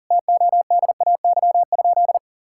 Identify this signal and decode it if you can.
Morse code, heard as a keyed tone, about 40 wpm, 700 Hz. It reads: TODAY?
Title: Sentences from Top 1000 Words - Review Part 6 - 40wpm